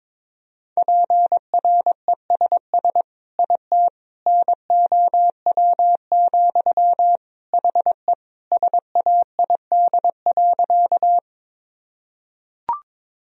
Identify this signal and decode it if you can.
Morse code, heard as a keyed tone, about 22 wpm, 700 Hz. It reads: PRESS IT NOW, HE SAID.
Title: Sentences from Top 1000 Words - Review Part 1 - 22wpm